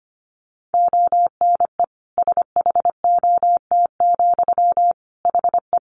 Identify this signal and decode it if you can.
Morse code, heard as a keyed tone, about 25 wpm, 700 Hz. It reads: ONE SHOT, HE
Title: Sentences from Top 1100 Words 25wpm